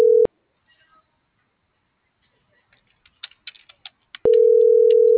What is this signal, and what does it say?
Ringback tone (North America), 440 + 480 Hz.